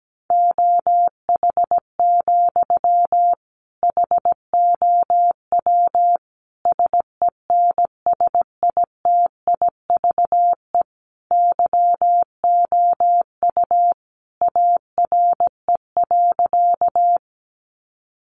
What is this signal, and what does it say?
Morse code, heard as a keyed tone, about 17 wpm, 700 Hz. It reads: OH, HOW SENSITIVE YOU ARE.
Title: Sentences from Top 1400 Words 17wpm